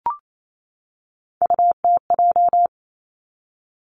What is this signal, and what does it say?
Morse code, heard as a keyed tone, about 28 wpm, 700 Hz. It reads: UTJ
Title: Three Challenging Letter-Number Combinations 28wpm